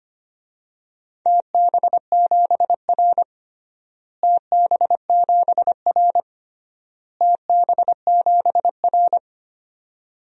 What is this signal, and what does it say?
Morse code, heard as a keyed tone, about 25 wpm, 700 Hz. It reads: TB7R TB7R TB7R
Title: DXCC-100 Call Signs - 4 Characters - Repeated 3x - 25wpm